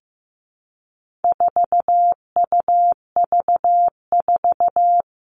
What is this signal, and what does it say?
Morse code, heard as a keyed tone, about 15 wpm, 700 Hz. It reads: 4UV4
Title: All Cut Numbers - 4-Digit 15wpm